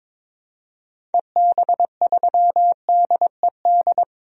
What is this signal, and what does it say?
Morse code, heard as a keyed tone, about 21 wpm, 700 Hz. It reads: EB3DED
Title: DXCC-100 Call Signs - 6 Characters - 22wpm